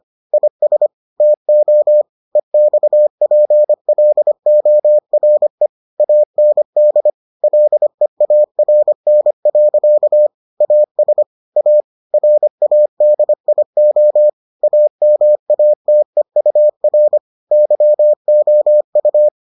Morse code, heard as a keyed tone, about 25 wpm, 600 Hz. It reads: IS TO EXPLORE AND LEARN. AS A RADIO AMATEUR YOU